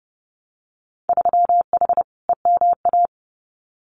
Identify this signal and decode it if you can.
Morse code, heard as a keyed tone, about 30 wpm, 700 Hz. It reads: 3H EMA
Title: Winter Field Day Exchanges Encore 30wpm